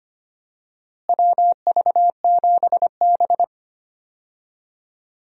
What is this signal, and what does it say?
Morse code, heard as a keyed tone, about 25 wpm, 700 Hz. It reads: WV7B
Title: USA Call Signs - 4 Characters - Repeated 2x - 25wpm